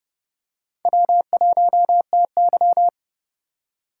Morse code, heard as a keyed tone, about 31 wpm, 700 Hz. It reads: W1TY
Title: USA Call Signs - 2x Character Spacing - 30wpm